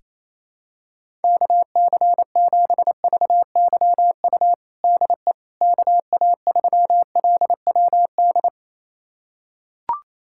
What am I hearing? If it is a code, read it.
Morse code, heard as a keyed tone, about 28 wpm, 700 Hz. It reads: KC7VYU DE KA3LWD
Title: USA Callsign DE Callsign 28wpm